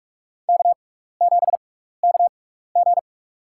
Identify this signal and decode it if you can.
Morse code, heard as a keyed tone, about 46 wpm, spaced as for 15 wpm, 700 Hz. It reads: K7KG
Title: USA Call Signs - 4 Characters - 6x Character Spacing - 45wpm